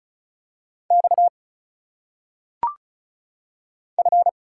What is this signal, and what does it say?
Morse code, heard as a keyed tone, about 34 wpm, 700 Hz. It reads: X F
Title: Single Letters - Phonetic Pronunciation 35wpm